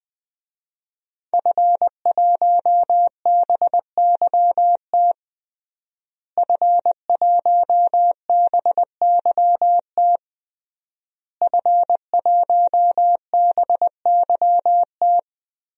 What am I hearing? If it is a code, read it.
Morse code, heard as a keyed tone, about 20 wpm, 700 Hz. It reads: F1BYT F1BYT F1BYT
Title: DXCC-100 Call Signs - 5 Characters - Repeated 3x - 20wpm